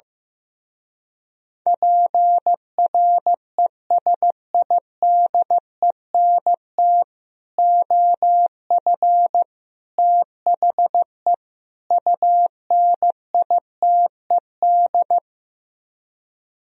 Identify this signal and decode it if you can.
Morse code, heard as a keyed tone, about 15 wpm, 700 Hz. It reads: PRESIDENT OF THE UNITED
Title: Speed-Racing - Sets of 4 Words 15wpm